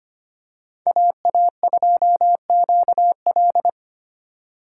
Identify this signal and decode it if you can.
Morse code, heard as a keyed tone, about 25 wpm, 700 Hz. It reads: AA2QL
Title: USA Call Signs - 5 Characters - 2x Character Spacing - 25wpm